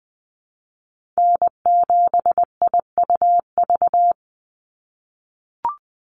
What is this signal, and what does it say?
Morse code, heard as a keyed tone, about 20 wpm, 700 Hz. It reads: N7IUV
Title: USA Call Signs 20wpm